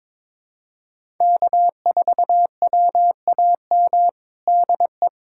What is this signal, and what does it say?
Morse code, heard as a keyed tone, about 22 wpm, 700 Hz. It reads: K4WAM DE
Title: USA Callsign DE Callsign - 5 Characters - 22wpm